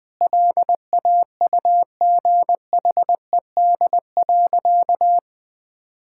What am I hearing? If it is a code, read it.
Morse code, heard as a keyed tone, about 20 wpm, 700 Hz. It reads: LAUGHED.
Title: Sentences with 2-Word Phrases - 20wpm